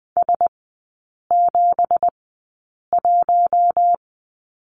Morse code, heard as a keyed tone, about 20 wpm, spaced as for 8 wpm, 700 Hz. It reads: S71
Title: 212 Lesson 34e - Number 8 - Three Characters 20wpm